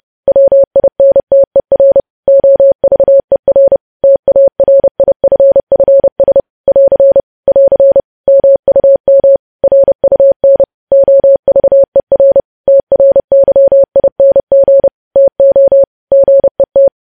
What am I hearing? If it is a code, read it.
Morse code, heard as a keyed tone, about 30 wpm, 550 Hz. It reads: WINTER OVER TARIFFS <AR> <AR> MUM RUN OVER TRYING TO GET
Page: news headlines in morse code at 30 wpm